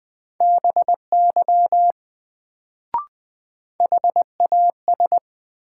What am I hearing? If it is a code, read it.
Morse code, heard as a keyed tone, about 21 wpm, 700 Hz. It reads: BY HAS
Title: Sets of 2 Words - New Word Drill - 20wpm